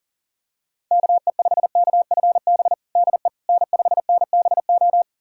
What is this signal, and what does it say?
Morse code, heard as a keyed tone, about 37 wpm, 700 Hz. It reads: KE5KFB DE N5NBO
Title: USA Callsign DE Callsign 40wpm